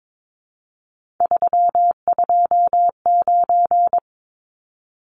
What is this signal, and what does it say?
Morse code, heard as a keyed tone, about 22 wpm, 700 Hz. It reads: 329
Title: Numbers - 3-Digit 22wpm